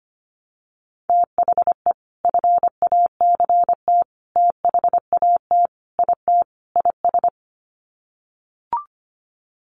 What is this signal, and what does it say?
Morse code, heard as a keyed tone, about 25 wpm, 700 Hz. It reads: THE FACT THAT IT IS
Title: Sets of 5 Words Encore2 25wpm